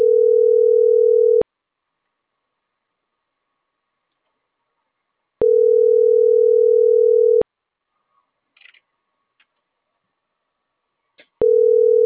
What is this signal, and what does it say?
Ringback tone (North America), 440 + 480 Hz.